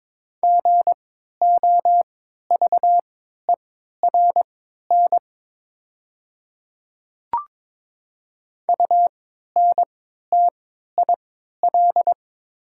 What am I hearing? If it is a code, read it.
Morse code, heard as a keyed tone, about 23 wpm, spaced as for 13 wpm, 690 Hz. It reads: GOVERN UNTIL
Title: 3x Character Spacing - Top 500 Words - 22wpm